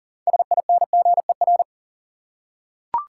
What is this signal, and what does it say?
Morse code, heard as a keyed tone, about 37 wpm, 700 Hz. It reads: SINGER
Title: Six Letter Words - 40wpm